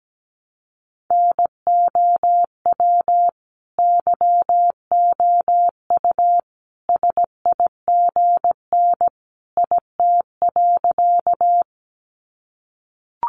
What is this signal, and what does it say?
Morse code, heard as a keyed tone, about 17 wpm, 700 Hz. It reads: NOW YOU SIGN IT.
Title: Sentences from Top 1000 Words 17wpm